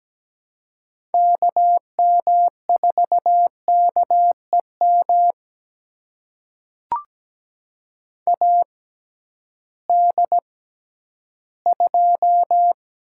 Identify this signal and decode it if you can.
Morse code, heard as a keyed tone, about 17 wpm, spaced as for 11 wpm, 700 Hz. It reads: KM4KEM AD2
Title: USA Call Signs - 6x Character Spacing - 17wpm